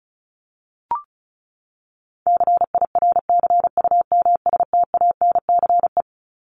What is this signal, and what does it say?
Morse code, heard as a keyed tone, about 34 wpm, 700 Hz. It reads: CIRCUMSTANCE
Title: ES:S Suffix - Top 100 Words with Base Words 35wpm